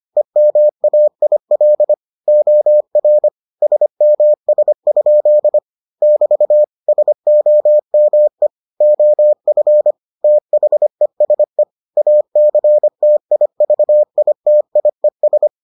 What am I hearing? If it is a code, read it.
Morse code, heard as a keyed tone, about 25 wpm, 600 Hz. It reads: EMAIL OR SMS? <BT> SOME OF THESE ACTIVITIES